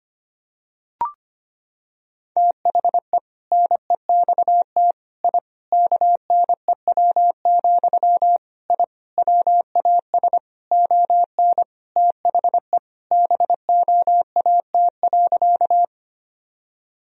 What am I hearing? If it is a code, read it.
Morse code, heard as a keyed tone, about 25 wpm, 700 Hz. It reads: THE NEXT I KNEW, I WAS ON THE BOAT.
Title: Sentences from Top 1000 Words - Review Part 3 - 25wpm